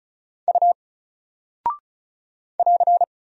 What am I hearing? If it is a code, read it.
Morse code, heard as a keyed tone, about 35 wpm, 700 Hz. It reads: U <AR>
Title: Single Character (Letter:Number:Punctuation:Prosign) 35wpm